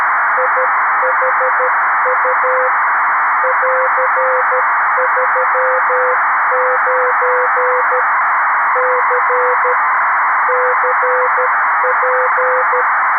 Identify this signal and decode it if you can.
Morse code, heard as a keyed tone, about 13 wpm, 510 Hz, with noise. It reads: IHU <AR>39 C CP
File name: arissat / recordings / ARISSat 11Sept2011 1359utc 2mins 14frames